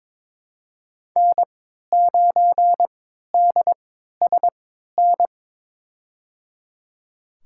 Morse code, heard as a keyed tone, about 22 wpm, spaced as for 13 wpm, 700 Hz. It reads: N9DSN
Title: USA Call Signs - 3x Character Spacing - 22wpm